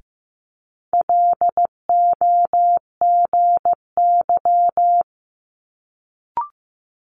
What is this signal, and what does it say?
Morse code, heard as a keyed tone, about 15 wpm, 700 Hz. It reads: LOGY
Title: Multi-letter Phonemes Quick Fire 15wpm